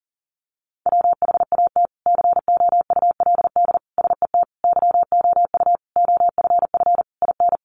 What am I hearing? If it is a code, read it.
Morse code, heard as a keyed tone, about 37 wpm, 700 Hz. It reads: WHAT COULD SET YOU OFF IN
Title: Sentences from Top 200 Words 40wpm